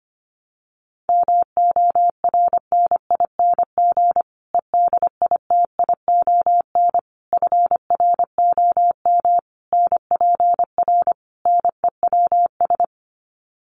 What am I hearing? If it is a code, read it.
Morse code, heard as a keyed tone, about 25 wpm, 700 Hz. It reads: MORNING EDITION FROM NPR NEWS
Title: Sets of 5 Words Encore 25wpm